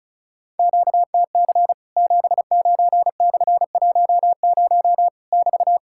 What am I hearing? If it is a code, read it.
Morse code, heard as a keyed tone, about 36 wpm, 700 Hz. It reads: QTC 79/10 <BT>
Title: Worked All Europe DX Contest - QTC Exchange - 35wpm